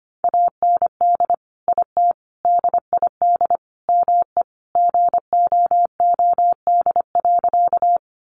Morse code, heard as a keyed tone, about 25 wpm, 700 Hz. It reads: AND IT DID ME GOOD.
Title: Sentences from Top 200 Words 25wpm